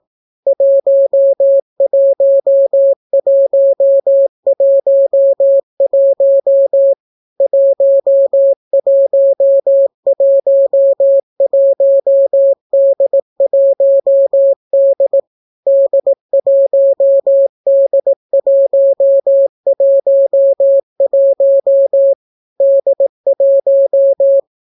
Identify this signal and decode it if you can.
Morse code, heard as a keyed tone, about 18 wpm, 550 Hz. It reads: 11111 1111D1D D1D111 D1